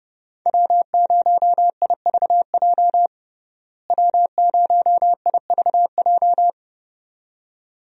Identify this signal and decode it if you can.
Morse code, heard as a keyed tone, about 32 wpm, 700 Hz. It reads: W0IVJ W0IVJ
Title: USA Call Signs - 5 Characters - Repeated 2x - 30wpm